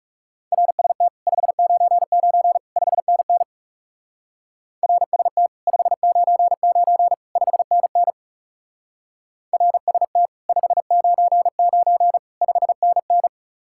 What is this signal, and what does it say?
Morse code, heard as a keyed tone, about 39 wpm, 700 Hz. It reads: RST 599 5NN RST 599 5NN RST 599 5NN